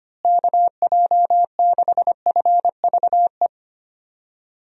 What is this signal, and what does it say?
Morse code, heard as a keyed tone, about 25 wpm, 700 Hz. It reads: KJ6FVE
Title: USA Call Signs - 6 Characters - Repeated 3x - 25wpm